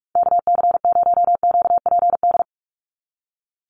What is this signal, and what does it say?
Morse code, heard as a keyed tone, about 44 wpm, 700 Hz. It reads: KC0QPD
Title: USA Call Signs 45wpm